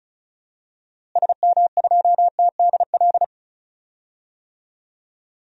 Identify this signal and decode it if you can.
Morse code, heard as a keyed tone, about 36 wpm, 700 Hz. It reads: SM2TDL
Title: DXCC-100 Call Signs - 6 Characters - Repeated 3x - 35wpm